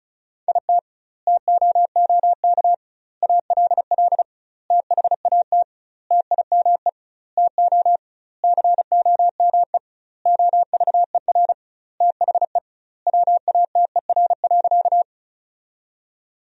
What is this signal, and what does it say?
Morse code, heard as a keyed tone, about 34 wpm, 700 Hz. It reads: IT TOOK ALL THAT TIME TO COME OVER THE WATER.
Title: 2x Word-Spacing - Sentences from Top 300 Words - 35wpm